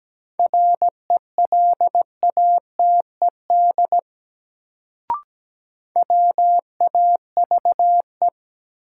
Morse code, heard as a keyed tone, about 17 wpm, 700 Hz. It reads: RELATED WAVE
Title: Sentences from Top 1400 Words - New Word Drill Alt - 17wpm